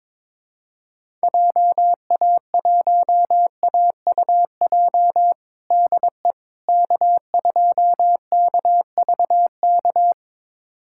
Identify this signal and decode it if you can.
Morse code, heard as a keyed tone, about 21 wpm, 700 Hz. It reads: JA1AUJ DE K2KVK
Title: USA Callsign DE Callsign 22wpm